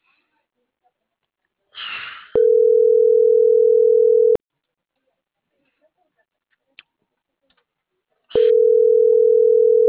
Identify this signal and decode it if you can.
Ringback tone (North America), 440 + 480 Hz.